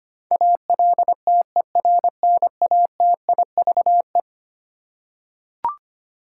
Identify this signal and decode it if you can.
Morse code, heard as a keyed tone, about 25 wpm, 700 Hz. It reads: ALTERNATIVE
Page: morse-code-ninja-practice-25wpm